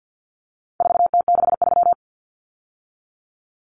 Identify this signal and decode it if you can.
Morse code, heard as a keyed tone, about 48 wpm, 700 Hz. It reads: 4T63
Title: Common Cut Numbers - 4-Digit 50wpm